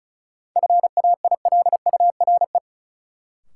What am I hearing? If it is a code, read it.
Morse code, heard as a keyed tone, about 36 wpm, 700 Hz. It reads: FAILURE
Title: Sentences from Top 900 Words - New Word Drill - 35wpm